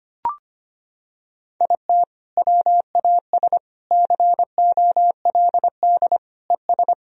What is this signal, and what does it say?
Morse code, heard as a keyed tone, about 25 wpm, 700 Hz. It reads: IT WAS COLD ES
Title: Sentences from Top 500 Words and Abbreviations 25wpm